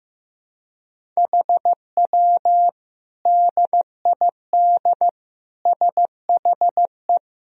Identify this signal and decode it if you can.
Morse code, heard as a keyed tone, about 15 wpm, 700 Hz. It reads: HW DID SHE
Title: Sentences from Top 100 Words and Abbreviations 15wpm